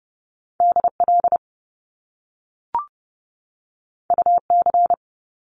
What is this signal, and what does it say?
Morse code, heard as a keyed tone, about 30 wpm, 700 Hz. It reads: DL UC
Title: Top Two Letter Combinations in Words 30wpm